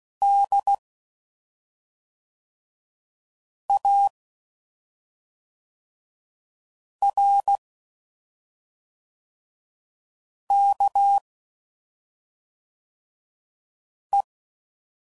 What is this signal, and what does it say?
Morse code, heard as a keyed tone, about 16 wpm, 800 Hz. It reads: D A R K E
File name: Suffixes / Special ING's / SlowToFaster ING's